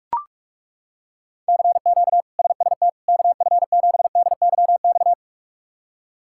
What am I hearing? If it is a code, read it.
Morse code, heard as a keyed tone, about 41 wpm, 700 Hz. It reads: CQ SST KF7DYX
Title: K1USN SST QSO Practice - 45wpm